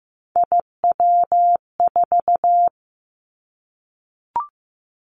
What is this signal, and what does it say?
Morse code, heard as a keyed tone, about 15 wpm, 700 Hz. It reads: IW4
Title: Callsign Prefixes for Top 25 DXCC - 15wpm